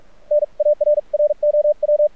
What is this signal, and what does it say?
Morse code, heard as a keyed tone, about 46 wpm, spaced as for 38 wpm, 590 Hz. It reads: NARROW
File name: archived / CRSnet